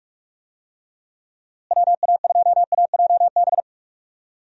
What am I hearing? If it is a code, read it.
Morse code, heard as a keyed tone, about 44 wpm, 700 Hz. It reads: WA2AJB
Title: USA Call Signs - 6 Characters - Repeated 3x - 45wpm